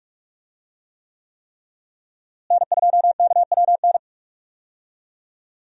Morse code, heard as a keyed tone, about 44 wpm, 700 Hz. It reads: N2KWN
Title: USA Call Signs - 5 Characters - Repeated 3x - 45wpm